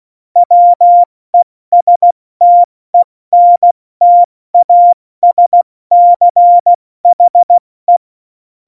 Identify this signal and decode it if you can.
Morse code, heard as a keyed tone, about 16 wpm, 700 Hz. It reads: WESTENTASCHE
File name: Morsekurs / Witze / Witz 17.Juli 2018, 70 F80